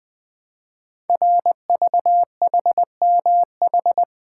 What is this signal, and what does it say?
Morse code, heard as a keyed tone, about 20 wpm, 700 Hz. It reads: RVHMH
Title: Code Groups - 5 Letters - 20wpm